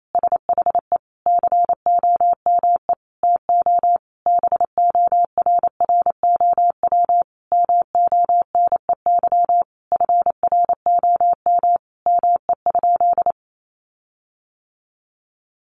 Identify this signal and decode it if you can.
Morse code, heard as a keyed tone, about 28 wpm, 700 Hz. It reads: SHE COME TO BORROW MONEY FROM ME?
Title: Sentences from Top 1500 Words 28wpm